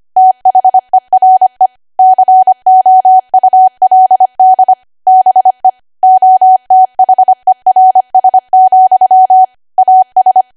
Morse code, heard as a keyed tone, about 25 wpm, 750 Hz. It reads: THERE COULD BE OTHERS, AS
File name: Archive / 25 WPM / 25WPM